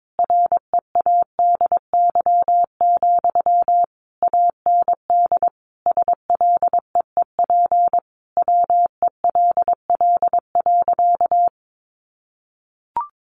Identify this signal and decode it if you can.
Morse code, heard as a keyed tone, about 22 wpm, 700 Hz. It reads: READY, AND SLEEP WELL.
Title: Sentences from Top 500 Words 22wpm